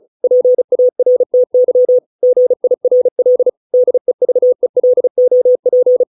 Morse code, heard as a keyed tone, about 34 wpm, 500 Hz. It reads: PARTY GIRL DEVELOP